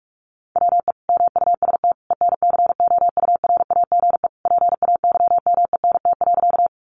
Morse code, heard as a keyed tone, about 46 wpm, 700 Hz. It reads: WE MUST ENCOURAGE PAYMENT.
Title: Sentences from Top 1400 Words 45wpm